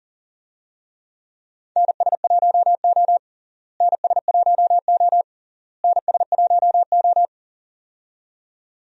Morse code, heard as a keyed tone, about 40 wpm, 700 Hz. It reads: NS1O NS1O NS1O
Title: USA Call Signs - 4 Characters - Repeated 3x - 40wpm